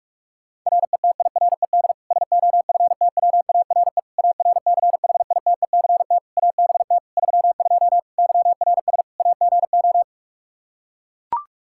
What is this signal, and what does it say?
Morse code, heard as a keyed tone, about 41 wpm, 700 Hz. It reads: RETIRED SOFTWARE ARCHITECT ABT 32 YRS AGO
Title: Practice QSO - Exchange 3 - More Info - 45wpm